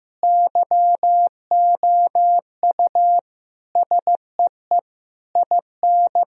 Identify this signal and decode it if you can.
Morse code, heard as a keyed tone, about 16 wpm, 700 Hz. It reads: YOU SEE IN
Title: Sentences from Top 1000 Words - Review Part 3 - 15wpm